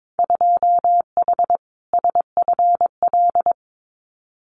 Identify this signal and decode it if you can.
Morse code, heard as a keyed tone, about 22 wpm, 700 Hz. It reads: 2H SFL
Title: Winter Field Day Exchanges Encore 22wpm